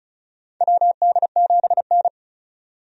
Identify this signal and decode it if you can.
Morse code, heard as a keyed tone, about 34 wpm, 700 Hz. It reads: WD7N
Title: USA Call Signs - 4 Characters - 35wpm